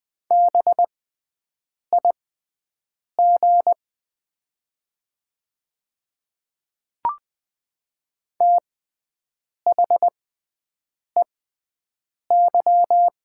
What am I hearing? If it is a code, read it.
Morse code, heard as a keyed tone, about 20 wpm, spaced as for 7 wpm, 700 Hz. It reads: BIG THEY
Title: 6x Character Spacing - Top 200 Words - 20wpm